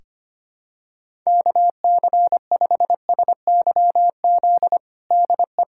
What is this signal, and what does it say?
Morse code, heard as a keyed tone, about 25 wpm, 700 Hz. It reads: KC5SYZ DE
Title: USA Callsign DE Callsign - 6 Characters - 25wpm